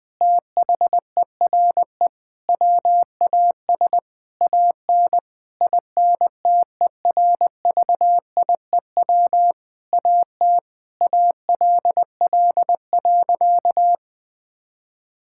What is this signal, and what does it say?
Morse code, heard as a keyed tone, about 20 wpm, 700 Hz. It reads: THERE WAS AN INTERVIEW AT ALL.
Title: Sentences from Top 1600 Words 20wpm